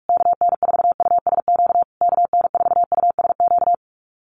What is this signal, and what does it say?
Morse code, heard as a keyed tone, about 44 wpm, 700 Hz. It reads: KN4USQ KN4USQ
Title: USA Call Signs Repeated 3x 45wpm